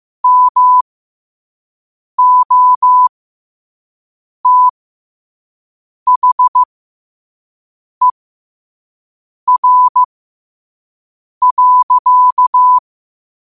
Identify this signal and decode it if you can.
Morse code, heard as a keyed tone, about 15 wpm, spaced as for 5 wpm, 1000 Hz. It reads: MOTHER.
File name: qotdfiles / 5wpm